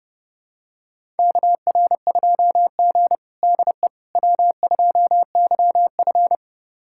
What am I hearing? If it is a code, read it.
Morse code, heard as a keyed tone, about 32 wpm, 700 Hz. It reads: KR2G DE W2YF
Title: USA Callsign DE Callsign - 4 Characters - 30wpm